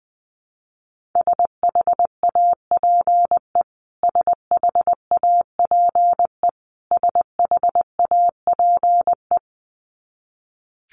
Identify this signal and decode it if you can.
Morse code, heard as a keyed tone, about 20 wpm, 700 Hz. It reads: SHAPE SHAPE SHAPE
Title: Top 500 Words Repeated 20wpm